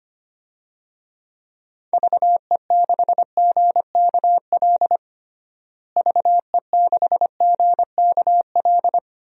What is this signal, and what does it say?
Morse code, heard as a keyed tone, about 25 wpm, 700 Hz. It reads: VE6GKL VE6GKL
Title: DXCC-100 Call Signs - 6 Characters - Repeated 2x - 25wpm